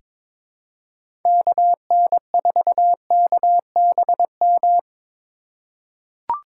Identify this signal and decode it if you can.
Morse code, heard as a keyed tone, about 22 wpm, 700 Hz. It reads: KN4KBM
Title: USA Call Signs - 6 Characters - 4x Character Spacing - 22wpm